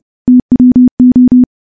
Morse code, heard as a keyed tone, about 28 wpm, 250 Hz. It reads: TWO
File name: morse / morse-training-30WPM